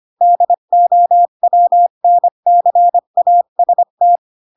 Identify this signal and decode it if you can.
Morse code, heard as a keyed tone, about 25 wpm, 700 Hz. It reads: DOWNCAST
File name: Pinocchio / Pinocchio ch8 20 25 ik1hge